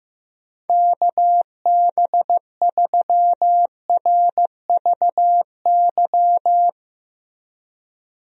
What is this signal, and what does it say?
Morse code, heard as a keyed tone, about 15 wpm, 700 Hz. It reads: KB3RVY
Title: USA Call Signs - 6 Characters - Repeated 3x - 15wpm